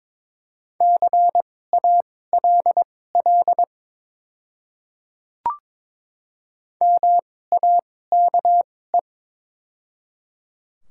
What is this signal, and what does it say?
Morse code, heard as a keyed tone, about 22 wpm, spaced as for 16 wpm, 700 Hz. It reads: CALL MAKE
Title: 2x Character Spacing - Top 200 Words - 22wpm